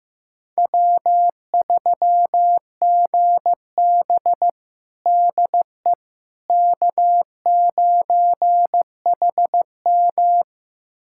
Morse code, heard as a keyed tone, about 15 wpm, 700 Hz. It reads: W3GB DE K9HM
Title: USA Callsign DE Callsign - 4 Characters - 15wpm